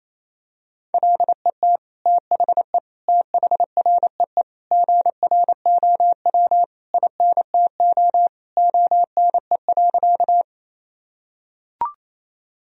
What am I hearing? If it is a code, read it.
Morse code, heard as a keyed tone, about 28 wpm, 700 Hz. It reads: LET THE THREE GROW INTO ONE.
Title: Sentences from Top 1000 Words - Review Part 1 - 28wpm